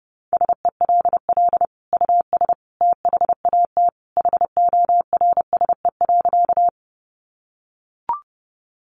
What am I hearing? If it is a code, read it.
Morse code, heard as a keyed tone, about 30 wpm, 700 Hz. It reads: SELL US THAT HORSE.
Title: Sentences from Top 1100 Words 30wpm